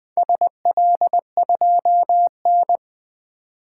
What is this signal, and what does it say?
Morse code, heard as a keyed tone, about 20 wpm, 700 Hz. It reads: SL2N
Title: Four Letter-Number Combinations - 20wpm